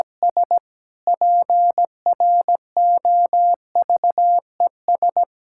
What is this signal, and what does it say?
Morse code, heard as a keyed tone, about 17 wpm, 700 Hz. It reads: S PROVES
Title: Pangram Challenge v2 - 17wpm